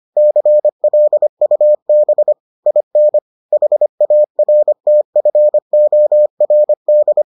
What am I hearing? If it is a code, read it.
Morse code, heard as a keyed tone, about 25 wpm, 600 Hz. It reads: CLUB IN HARTFORD